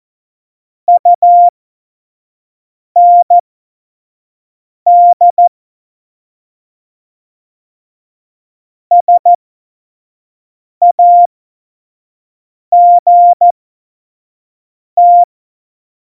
Morse code, heard as a keyed tone, about 14 wpm, spaced as for 5 wpm, 700 Hz. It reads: UND SAGT
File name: Morsekurs / Witze / Witz 12. Dezember 2017, 25 F70